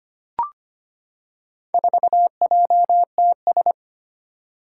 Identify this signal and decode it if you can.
Morse code, heard as a keyed tone, about 25 wpm, 700 Hz. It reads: 4JTS
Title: Four Challenging Letter-Number Combinations 25wpm